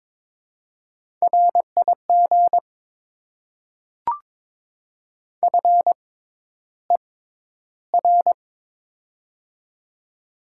Morse code, heard as a keyed tone, about 22 wpm, spaced as for 11 wpm, 700 Hz. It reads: RIG FER
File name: Top 20 QSO Elements - 6x Character Spacing - 22wpm